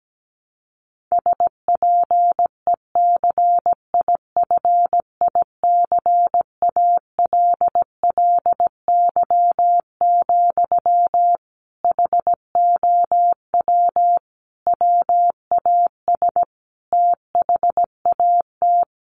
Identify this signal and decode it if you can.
Morse code, heard as a keyed tone, about 17 wpm, 700 Hz. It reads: SPECIFICALLY, HOW WAS THAT
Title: Sentences from Top 1100 Words 17wpm